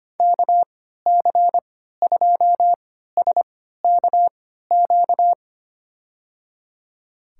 Morse code, heard as a keyed tone, about 25 wpm, spaced as for 14 wpm, 700 Hz. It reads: KC2SKQ
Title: USA Call Signs - 6 Characters - 3x Character Spacing - 25wpm